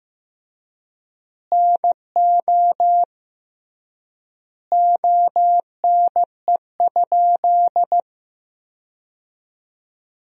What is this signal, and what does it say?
Morse code, heard as a keyed tone, about 15 wpm, 700 Hz. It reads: NO ONE?
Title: 3x Word-Spacing - Sentences from Top 300 Words - 15wpm